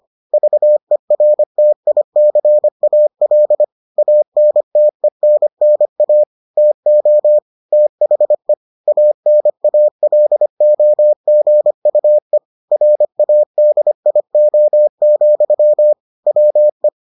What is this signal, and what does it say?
Morse code, heard as a keyed tone, about 25 wpm, 600 Hz. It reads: VERTICAL ANTENNA TO THE ANALOGUE RADIO, WE